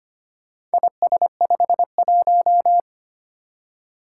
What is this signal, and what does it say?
Morse code, heard as a keyed tone, about 25 wpm, 700 Hz. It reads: IS51
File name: practice / 25wpm / Four Letter-Number Combinations - 25wpm